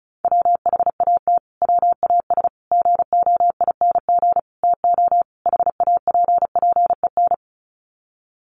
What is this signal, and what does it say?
Morse code, heard as a keyed tone, about 34 wpm, 700 Hz. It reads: WHAT WAS GOING TO HAPPEN
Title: Sets of 5 Words Encore 35wpm